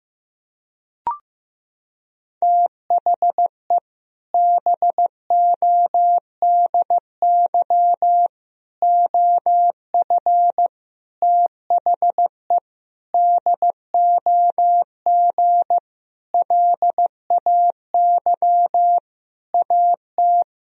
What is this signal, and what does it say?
Morse code, heard as a keyed tone, about 15 wpm, 700 Hz. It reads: THE BODY OF THE DOG LAY AT
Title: Sentences from Top 1000 Words - Review Part 2 - 15wpm